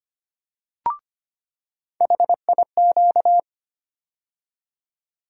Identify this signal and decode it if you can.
Morse code, heard as a keyed tone, about 25 wpm, 700 Hz. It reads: HIQ
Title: USA Call Sign Postfixes - 3 Letters 25wpm